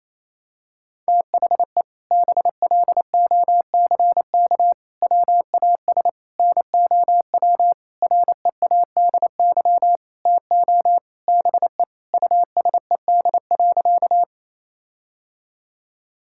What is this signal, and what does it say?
Morse code, heard as a keyed tone, about 28 wpm, 700 Hz. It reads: THE BLOCK WAS NOW READY TO BE USED.
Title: Sentences from Top 1600 Words 28wpm